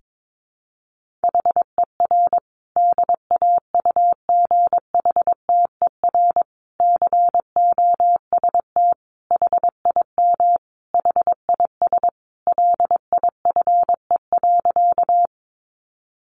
Morse code, heard as a keyed tone, about 22 wpm, 700 Hz. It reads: HER DAUGHTER COST HIM HIS LIFE.
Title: Sentences from Top 1500 Words 22wpm